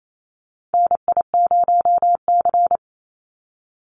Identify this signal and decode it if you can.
Morse code, heard as a keyed tone, about 28 wpm, 700 Hz. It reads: NI0C
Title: Call Signs Repeated 2x - 28wpm